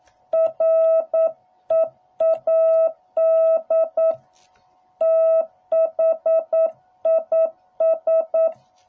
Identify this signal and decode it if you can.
Morse code, heard as a keyed tone, about 9 wpm, 660 Hz. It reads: READ THIS